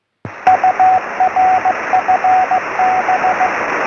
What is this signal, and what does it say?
Morse code, heard as a keyed tone, about 17 wpm, 740 Hz, with radio noise. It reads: UR FB